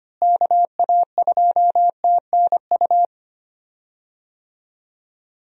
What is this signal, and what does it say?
Morse code, heard as a keyed tone, about 25 wpm, 700 Hz. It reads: KA2TNU
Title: USA Call Signs - 6 Characters - Repeated 2x - 25wpm